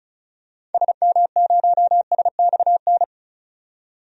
Morse code, heard as a keyed tone, about 34 wpm, 700 Hz. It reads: SM0SXN
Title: DXCC-100 Call Signs - 6 Characters - 35wpm